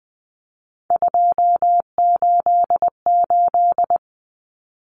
Morse code, heard as a keyed tone, about 20 wpm, 700 Hz. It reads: 288
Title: Common Cut Numbers - 3-Digit 20wpm